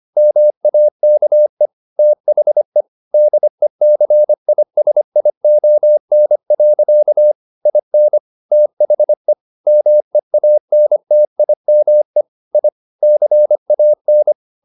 Morse code, heard as a keyed tone, about 25 wpm, 600 Hz. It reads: MAKE THE DECISION. IN THE MEANTIME I CAN